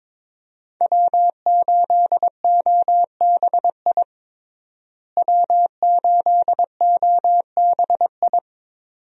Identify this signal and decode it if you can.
Morse code, heard as a keyed tone, about 22 wpm, 700 Hz. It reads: W8OBI W8OBI
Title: USA Call Signs - 5 Characters - Repeated 2x - 22wpm